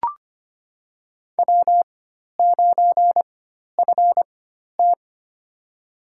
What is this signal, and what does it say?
Morse code, heard as a keyed tone, about 25 wpm, spaced as for 12 wpm, 700 Hz. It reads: W9FT
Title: USA Call Signs - 4 Characters - 4x Character Spacing - 25wpm